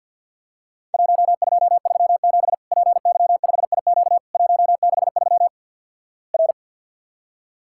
Morse code, heard as a keyed tone, about 48 wpm, 700 Hz. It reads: 1237 PY5IQ 163 R
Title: Worked All Europe DX Contest - QTC Exchange with Repeats - 50wpm